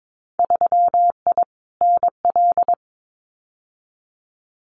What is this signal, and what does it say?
Morse code, heard as a keyed tone, about 22 wpm, 700 Hz. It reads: 3I NL
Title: Winter Field Day Exchanges 22wpm